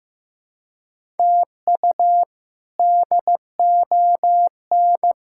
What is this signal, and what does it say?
Morse code, heard as a keyed tone, about 15 wpm, 700 Hz. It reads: TU DON
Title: K1USN SST QSO Practice - 15wpm